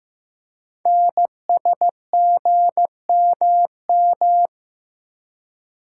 Morse code, heard as a keyed tone, about 15 wpm, 700 Hz. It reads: NSGMM
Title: Code Groups - 5 Letters - 15wpm